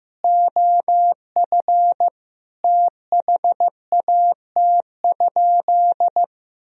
Morse code, heard as a keyed tone, about 16 wpm, 700 Hz. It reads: OF THAT?
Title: Sentences from Top 200 Words and Abbreviations 15wpm